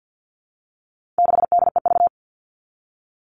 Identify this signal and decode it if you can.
Morse code, heard as a keyed tone, about 50 wpm, 700 Hz. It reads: 6DEV
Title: All Cut Numbers - 4-Digit 50wpm